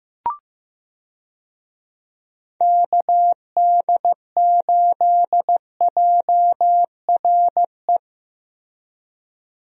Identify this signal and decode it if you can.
Morse code, heard as a keyed tone, about 15 wpm, 700 Hz. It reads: KD8JRE
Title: USA Call Signs - 6 Characters - Repeated 3x - 15wpm